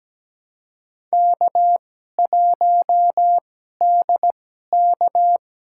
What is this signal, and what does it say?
Morse code, heard as a keyed tone, about 17 wpm, spaced as for 12 wpm, 700 Hz. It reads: K1DK
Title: USA Call Signs - 4 Characters - 2x Character Spacing - 17wpm